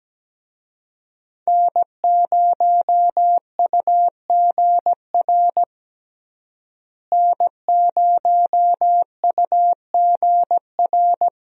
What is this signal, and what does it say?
Morse code, heard as a keyed tone, about 17 wpm, 700 Hz. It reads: N0UGR N0UGR
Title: USA Call Signs - 5 Characters - Repeated 3x - 17wpm